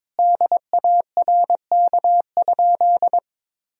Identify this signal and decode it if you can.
Morse code, heard as a keyed tone, about 22 wpm, 700 Hz. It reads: DARK?
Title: Sentences from Top 500 Words and Abbreviations 22wpm